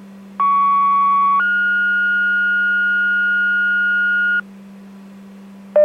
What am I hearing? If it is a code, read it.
Two-tone page, heard as a one-second tone, then a three-second tone, about 1120 then 1430 Hz, with noise. It is heard from 0.4 to 4.4 s.